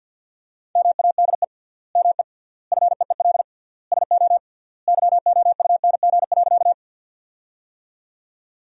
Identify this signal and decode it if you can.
Morse code, heard as a keyed tone, about 50 wpm, 700 Hz. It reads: MADE ME FEEL SO YOUNG.
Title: 3x Word-Spacing - Sentences from Top 400 Words - 50wpm